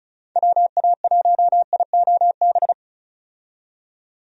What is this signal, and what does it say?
Morse code, heard as a keyed tone, about 36 wpm, 700 Hz. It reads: WA1IOB
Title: USA Call Signs - 6 Characters - 35wpm